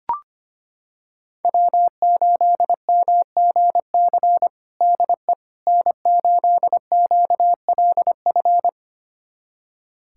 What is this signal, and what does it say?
Morse code, heard as a keyed tone, about 25 wpm, 700 Hz. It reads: W8MGC DE N8QLF
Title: USA Callsign DE Callsign - 5 Characters - 25wpm